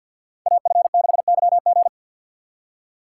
Morse code, heard as a keyed tone, about 52 wpm, 700 Hz. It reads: AF6YG
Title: USA Call Signs - 5 Characters - 6x Character Spacing - 50wpm